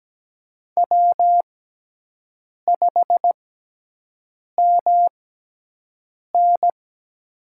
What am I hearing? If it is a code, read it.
Morse code, heard as a keyed tone, about 17 wpm, spaced as for 6 wpm, 700 Hz. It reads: W5MN
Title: USA Call Signs - 4 Characters - 6x Character Spacing - 17wpm